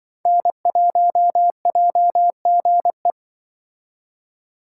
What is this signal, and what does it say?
Morse code, heard as a keyed tone, about 24 wpm, 700 Hz. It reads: N1JGE
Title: Speed Racing - USA 1x3 Call Signs 20wpm